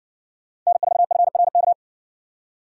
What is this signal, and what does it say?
Morse code, heard as a keyed tone, about 57 wpm, 700 Hz. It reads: N4FRK